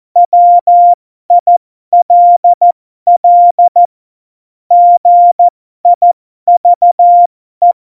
Morse code, heard as a keyed tone, about 14 wpm, spaced as for 12 wpm, 700 Hz. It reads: WILL GIVE
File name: Morsekurs / Witze